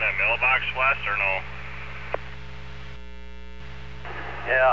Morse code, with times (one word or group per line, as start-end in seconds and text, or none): none